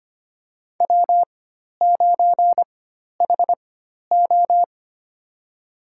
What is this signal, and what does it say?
Morse code, heard as a keyed tone, about 25 wpm, spaced as for 12 wpm, 700 Hz. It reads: W9HO
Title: USA Call Signs - 4 Characters - 4x Character Spacing - 25wpm